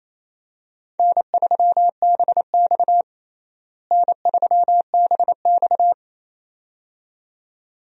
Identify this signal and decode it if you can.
Morse code, heard as a keyed tone, about 28 wpm, 700 Hz. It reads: N3BX N3BX
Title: USA Call Signs - 4 Characters - Repeated 2x - 28wpm